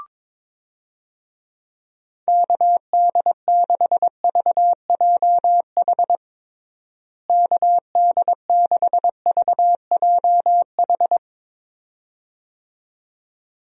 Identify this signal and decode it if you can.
Morse code, heard as a keyed tone, about 22 wpm, 700 Hz. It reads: KD6VJH KD6VJH
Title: USA Call Signs - 6 Characters - Repeated 2x - 22wpm